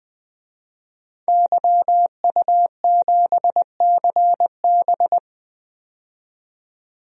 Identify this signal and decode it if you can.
Morse code, heard as a keyed tone, about 20 wpm, 700 Hz. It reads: YU7CB
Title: DXCC-100 Call Signs - 5 Characters - 20wpm